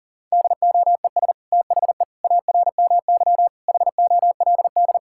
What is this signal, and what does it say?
Morse code, heard as a keyed tone, about 40 wpm, 690 Hz. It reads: DOES THE ARMY HOLD
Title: Sentences from Top 1000 Words - Review Part 6 - 40wpm